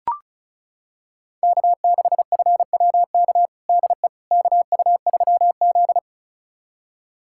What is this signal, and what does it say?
Morse code, heard as a keyed tone, about 36 wpm, 700 Hz. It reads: K6FWK DE KU3Z
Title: USA Callsign DE Callsign 35wpm